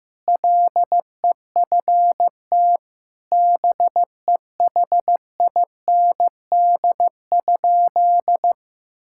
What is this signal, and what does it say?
Morse code, heard as a keyed tone, about 15 wpm, 700 Hz. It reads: LEFT BEHIND?
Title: Sentences from Top 1000 Words - Review Part 5 - 15wpm